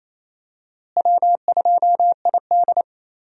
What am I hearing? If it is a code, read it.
Morse code, heard as a keyed tone, about 28 wpm, 700 Hz. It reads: W2ID
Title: USA Call Signs - 4 Characters - Repeated 3x - 28wpm